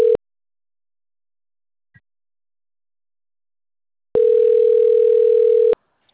Ringback tone (North America), 440 + 480 Hz.